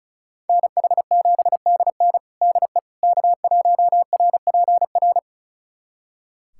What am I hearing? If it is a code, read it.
Morse code, heard as a keyed tone, about 36 wpm, 700 Hz. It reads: NH7DN DE K1RPR
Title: USA Callsign DE Callsign 35wpm